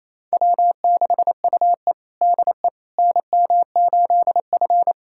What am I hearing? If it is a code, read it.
Morse code, heard as a keyed tone, about 28 wpm, 700 Hz. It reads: W6UE DE NM8F
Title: USA Callsign DE Callsign - 4 Characters - 28wpm